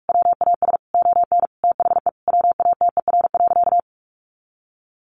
Morse code, heard as a keyed tone, about 44 wpm, 700 Hz. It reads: WAS ON THE WATER.
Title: Sentences from Top 500 Words 45wpm